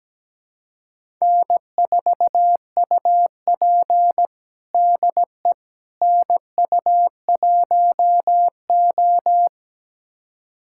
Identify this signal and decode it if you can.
Morse code, heard as a keyed tone, about 17 wpm, 700 Hz. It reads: N4UP DE NU1O
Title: USA Callsign DE Callsign - 4 Characters - 17wpm